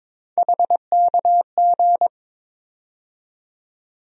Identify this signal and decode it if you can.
Morse code, heard as a keyed tone, about 22 wpm, 700 Hz. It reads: HKG
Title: USA Call Sign Postfixes - 3 Letters 22wpm